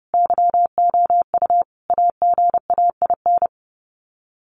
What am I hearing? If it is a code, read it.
Morse code, heard as a keyed tone, about 30 wpm, 700 Hz. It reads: YOU AGAIN
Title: Sets of 5 Words Encore2 30wpm